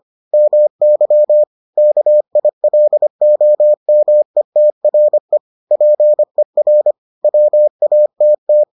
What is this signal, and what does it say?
Morse code, heard as a keyed tone, about 25 wpm, 600 Hz. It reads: MY KILOMETRE PER WATT